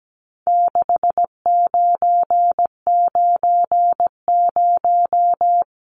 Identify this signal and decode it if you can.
Morse code, heard as a keyed tone, about 17 wpm, 700 Hz. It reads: 6990
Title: Numbers - 4-Digit 17wpm